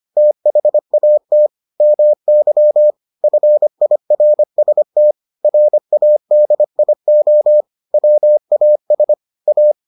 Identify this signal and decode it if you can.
Morse code, heard as a keyed tone, about 25 wpm, 600 Hz. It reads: THAT MY FIRST RADIO WAS A